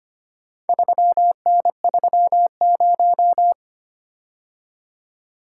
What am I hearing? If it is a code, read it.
Morse code, heard as a keyed tone, about 25 wpm, 700 Hz. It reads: 3N30
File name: practice / 25wpm / Four Letter-Number Combinations - 25wpm